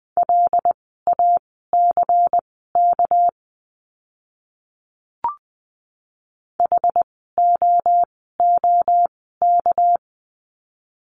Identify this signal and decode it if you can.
Morse code, heard as a keyed tone, about 20 wpm, spaced as for 14 wpm, 700 Hz. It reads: LACK HOOK
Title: 199 Lesson 32f - Letter k - Words 20wpm